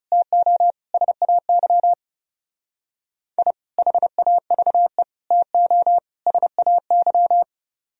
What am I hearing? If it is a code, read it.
Morse code, heard as a keyed tone, about 34 wpm, 700 Hz. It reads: TO SAY I HAVE TO SAY
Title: Speed-Racing - Sets of 4 Words 30wpm